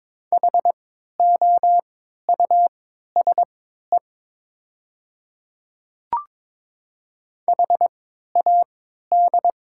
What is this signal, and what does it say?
Morse code, heard as a keyed tone, about 22 wpm, spaced as for 12 wpm, 700 Hz. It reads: HOUSE HAD
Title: 3x Character Spacing - Top 300 Words - 22wpm